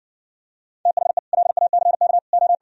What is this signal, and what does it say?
Morse code, heard as a keyed tone, about 60 wpm, 700 Hz. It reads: THE LACK O